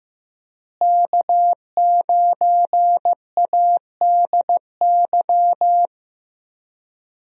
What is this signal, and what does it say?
Morse code, heard as a keyed tone, about 16 wpm, 700 Hz. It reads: K9ADY
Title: USA Call Signs - 5 Characters - Repeated 3x - 15wpm